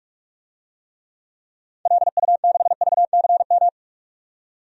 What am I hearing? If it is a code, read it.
Morse code, heard as a keyed tone, about 43 wpm, 700 Hz. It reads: LU6VCM
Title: DXCC-100 Call Signs - 6 Characters - 45wpm